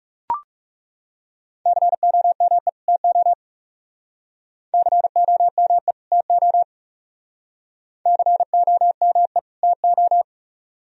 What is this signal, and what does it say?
Morse code, heard as a keyed tone, about 34 wpm, 700 Hz. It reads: COME TO COME TO COME TO
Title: Speed-Racing - Sets of 2 Words 30wpm